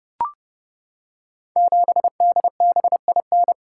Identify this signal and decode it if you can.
Morse code, heard as a keyed tone, about 30 wpm, 700 Hz. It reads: 7DBIN
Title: Code Groups - 5 Letter:Numbers - 30wpm